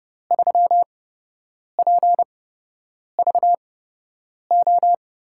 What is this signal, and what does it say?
Morse code, heard as a keyed tone, about 31 wpm, spaced as for 8 wpm, 700 Hz. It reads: 3PVO
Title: USA Call Signs - 6 Characters - 8x Character Spacing - 30wpm